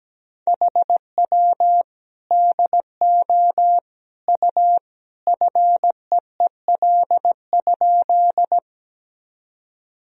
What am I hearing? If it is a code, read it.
Morse code, heard as a keyed tone, about 17 wpm, 700 Hz. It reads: HW DO U FEEL?
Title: Sentences from Top 400 Words and Abbreviations 17wpm